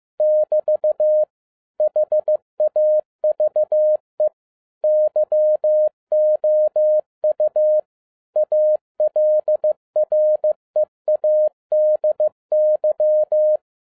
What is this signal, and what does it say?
Morse code, heard as a keyed tone, about 15 wpm, 610 Hz. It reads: <BT> HAVE YOU ALREADY
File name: 2011-10-mp3 / n12-JL3YMV 15wpm